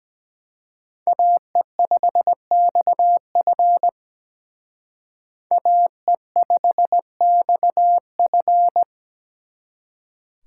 Morse code, heard as a keyed tone, about 20 wpm, 700 Hz. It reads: AE5XF AE5XF
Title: Speed Racing - USA Call Signs 17wpm